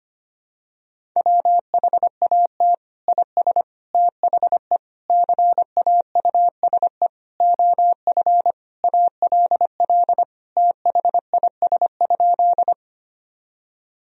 Morse code, heard as a keyed tone, about 25 wpm, 700 Hz. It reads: WHAT IS THE CAUSE OF ALL THIS?
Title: Sentences from Top 1000 Words - Review Part 1 - 25wpm